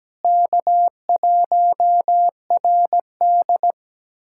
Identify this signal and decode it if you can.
Morse code, heard as a keyed tone, about 17 wpm, 700 Hz. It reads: K1RD
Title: USA Callsign DE Callsign - 4 Characters - 17wpm